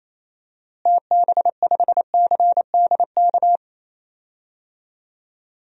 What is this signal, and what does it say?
Morse code, heard as a keyed tone, about 28 wpm, 700 Hz. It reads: TB5CDK
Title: DXCC-100 Call Signs - 6 Characters - Repeated 2x - 28wpm